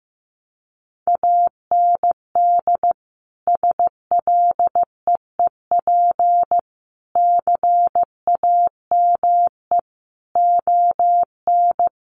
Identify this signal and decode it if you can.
Morse code, heard as a keyed tone, about 15 wpm, 700 Hz. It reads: AND SLEEP CAME ON
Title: Sentences from Top 1500 Words 15wpm